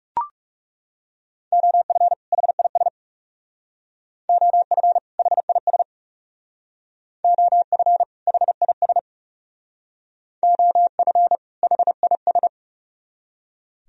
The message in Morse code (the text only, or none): OF HIS OF HIS OF HIS OF HIS